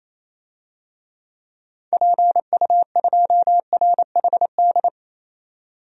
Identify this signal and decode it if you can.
Morse code, heard as a keyed tone, about 28 wpm, 700 Hz. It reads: PU2RHD
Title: DXCC-100 Call Signs - 6 Characters - 28wpm